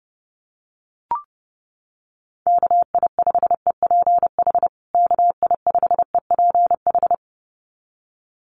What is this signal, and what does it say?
Morse code, heard as a keyed tone, about 30 wpm, 700 Hz. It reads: KI5EPH KI5EPH
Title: USA Call Signs Repeated 2x 30wpm